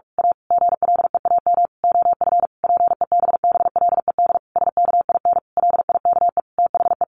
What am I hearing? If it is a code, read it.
Morse code, heard as a keyed tone, about 46 wpm, 700 Hz. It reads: A GLEAM OF PEBBLED SKIN LIKE THE
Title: A Sound Of Thunder 45wpm